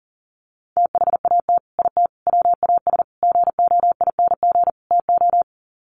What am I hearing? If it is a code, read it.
Morse code, heard as a keyed tone, about 40 wpm, 700 Hz. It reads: THAT IT WAS GOING TO
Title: Sets of 5 Words Encore2 40wpm